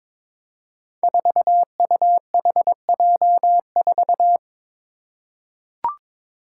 Morse code, heard as a keyed tone, about 22 wpm, 700 Hz. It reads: 4UHJ4
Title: Five Challenging Letter-Number Combinations 22wpm